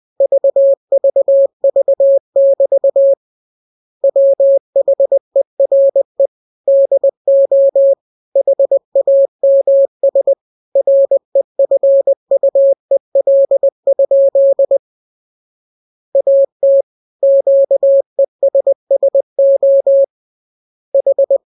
Morse code, heard as a keyed tone, about 20 wpm, 550 Hz. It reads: VVV<BT> WHERE DO HAMS REFUEL? AT QESSO H
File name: jokes / joke 22 20 0000